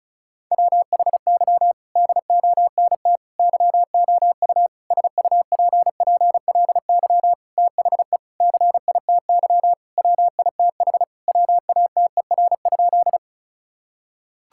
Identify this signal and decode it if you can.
Morse code, heard as a keyed tone, about 36 wpm, 700 Hz. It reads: WHY DONT YOU SUPPLY THE CITY WITH WATER?
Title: Sentences from Top 1000 Words - Review Part 5 - 35wpm